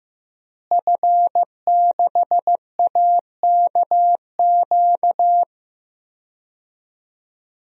Morse code, heard as a keyed tone, about 15 wpm, 700 Hz. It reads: F6AKQ
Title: DXCC-100 Call Signs - 5 Characters - Repeated 3x - 15wpm